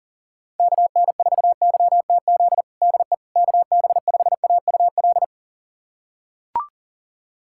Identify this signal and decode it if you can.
Morse code, heard as a keyed tone, about 37 wpm, 700 Hz. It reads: KN4YTZ DE KB5AUL
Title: USA Callsign DE Callsign - 6 Characters - 40wpm